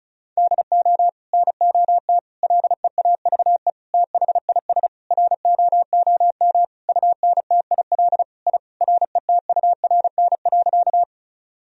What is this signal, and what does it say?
Morse code, heard as a keyed tone, about 34 wpm, 700 Hz. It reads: DO NOT LEAVE THIS ROOM UNTIL I RETURN.
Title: Sentences from Top 1000 Words - Review Part 1 - 35wpm